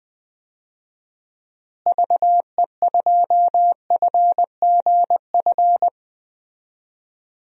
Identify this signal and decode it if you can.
Morse code, heard as a keyed tone, about 20 wpm, 700 Hz. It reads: VE2FGF
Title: DXCC-100 Call Signs - 6 Characters - Repeated 2x - 20wpm